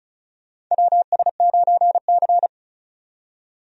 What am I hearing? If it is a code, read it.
Morse code, heard as a keyed tone, about 34 wpm, 700 Hz. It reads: WS9C
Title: USA Call Signs - 4 Characters - 35wpm